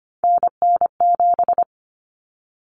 Morse code, heard as a keyed tone, about 25 wpm, 700 Hz. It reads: NN7
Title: Common Cut Numbers - 3-Digit 25wpm